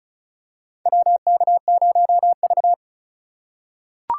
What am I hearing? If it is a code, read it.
Morse code, heard as a keyed tone, about 36 wpm, 700 Hz. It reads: WK0V
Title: USA Call Signs - 2x Character Spacing - 35wpm